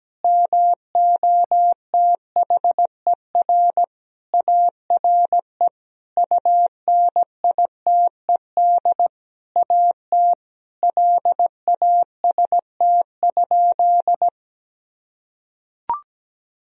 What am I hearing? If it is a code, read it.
Morse code, heard as a keyed tone, about 17 wpm, 700 Hz. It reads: MOTHER ARE UNITED AT LAST?
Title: Sentences from Top 1000 Words - Review Part 4 - 17wpm